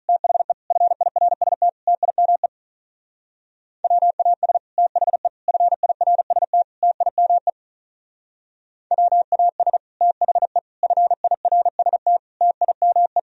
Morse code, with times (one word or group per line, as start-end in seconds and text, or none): 0.1-0.5 THE
0.7-1.7 FIRST
1.9-2.5 TIME
3.8-4.6 WAS
4.8-5.3 THE
5.5-6.6 FIRST
6.8-7.5 TIME
8.9-9.8 WAS
10.0-10.6 THE
10.8-12.2 FIRST
12.4-13.2 TIME